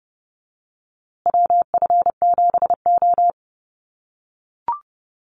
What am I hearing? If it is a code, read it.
Morse code, heard as a keyed tone, about 30 wpm, 700 Hz. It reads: WF7O
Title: USA Call Signs Repeated 2x 30wpm